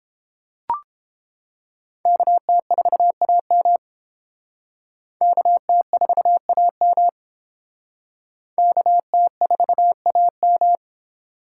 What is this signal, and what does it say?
Morse code, heard as a keyed tone, about 28 wpm, 700 Hz. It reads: KT4AM KT4AM KT4AM